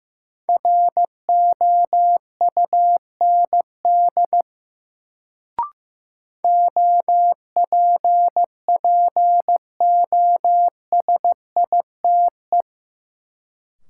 Morse code, heard as a keyed tone, about 15 wpm, 700 Hz. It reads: ROUND OPPOSITE
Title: Sets of 5 Words Encore - New Word Drill - 15wpm